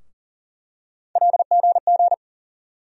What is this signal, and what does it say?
Morse code, heard as a keyed tone, about 40 wpm, 700 Hz. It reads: LGG
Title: USA Call Sign Postfixes - 3 Letters 40wpm